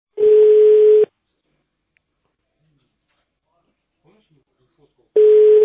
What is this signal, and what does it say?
Ringback tone (Europe, 425 Hz), with speech.
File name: 1011. .e48f63dd-794b-4482-b9ac